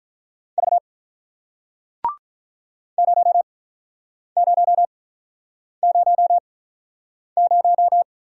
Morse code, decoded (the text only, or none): V 0000